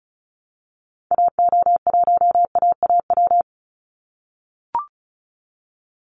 Morse code, heard as a keyed tone, about 34 wpm, 700 Hz. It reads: AO1AAW
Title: Call Signs - 35wpm